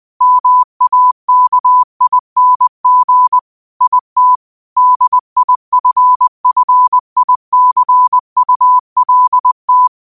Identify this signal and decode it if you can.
Morse code, heard as a keyed tone, about 20 wpm, 1000 Hz. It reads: MAKING IT DIFFICULT
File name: qotdfiles / 20wpm